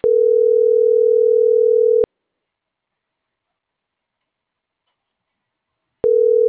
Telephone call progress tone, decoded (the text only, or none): ringback tone